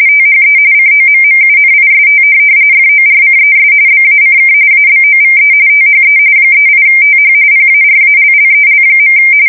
RTTY, about 45.45 baud. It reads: THE COMMUNIST FORCES BACK ABOVE THE 38TH PARALLEL.
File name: fsk170 / USS-New-Jersey